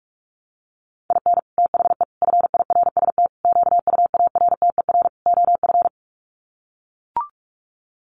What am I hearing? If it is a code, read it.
Morse code, heard as a keyed tone, about 44 wpm, 700 Hz. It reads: IN THE FIRST QUARTER OF
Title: Sets of 5 Words Encore2 45wpm